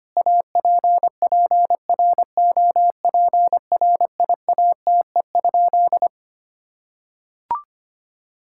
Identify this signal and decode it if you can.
Morse code, heard as a keyed tone, about 25 wpm, 700 Hz. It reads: APPROPRIATE?
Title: Sentences from Top 1000 Words - Review Part 5 - 25wpm